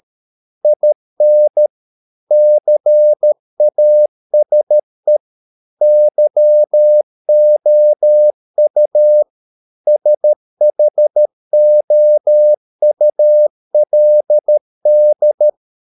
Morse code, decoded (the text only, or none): IN CASE YOU SHOULD